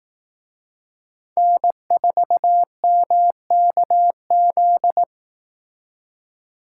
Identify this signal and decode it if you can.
Morse code, heard as a keyed tone, about 18 wpm, 700 Hz. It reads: N4MKZ
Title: Speed Racing - USA 1x3 Call Signs 15wpm